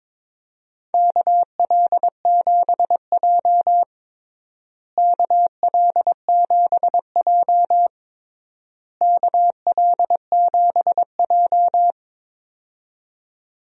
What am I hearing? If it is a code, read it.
Morse code, heard as a keyed tone, about 22 wpm, 700 Hz. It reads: KL7J KL7J KL7J
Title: DXCC-100 Call Signs - 4 Characters - Repeated 3x - 22wpm